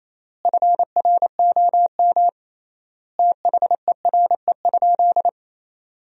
Morse code, heard as a keyed tone, about 28 wpm, 700 Hz. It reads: FROM THERE?
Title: 3x Word-Spacing - Sentences from Top 100 Words - 28wpm